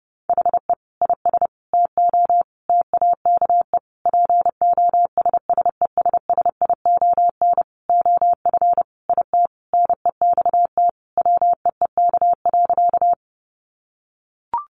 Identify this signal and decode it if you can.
Morse code, heard as a keyed tone, about 30 wpm, 700 Hz. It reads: HE IS TO TAKE POSSESSION OF IT NEXT WEEK.
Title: Sentences from Top 1400 Words 30wpm